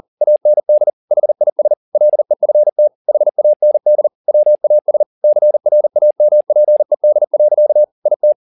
Morse code, heard as a keyed tone, about 40 wpm, 600 Hz. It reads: AND HIS LEFT HAND WAS CRAMPED. IT